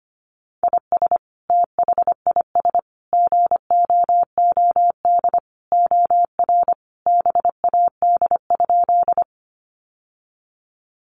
Morse code, heard as a keyed tone, about 25 wpm, 700 Hz. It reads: IS THIS GOOD OR BAD?
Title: Sentences from Top 700 Words 25wpm